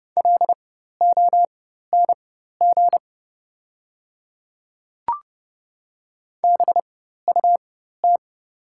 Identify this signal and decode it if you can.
Morse code, heard as a keyed tone, about 30 wpm, spaced as for 14 wpm, 700 Hz. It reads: LONG BUT
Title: 4x Character Spacing - Top 100 Words - 30wpm